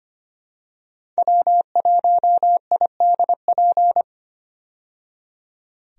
Morse code, heard as a keyed tone, about 25 wpm, 700 Hz. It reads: W1IDP
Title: USA Call Signs - 5 Characters - 25wpm